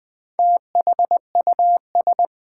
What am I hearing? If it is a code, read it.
Morse code, heard as a keyed tone, about 20 wpm, 700 Hz. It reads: THUS
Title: Build A Word - 2 Letters At A Time - 4-Letter Words - 20wpm